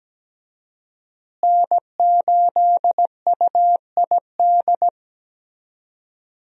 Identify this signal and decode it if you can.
Morse code, heard as a keyed tone, about 17 wpm, 700 Hz. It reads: N8UID
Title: USA Call Signs - 5 Characters - Repeated 2x - 17wpm